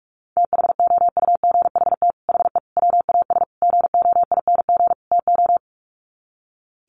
Morse code, heard as a keyed tone, about 44 wpm, 700 Hz. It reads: THOUGHT HE WAS GOING TO
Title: Sets of 5 Words Encore2 45wpm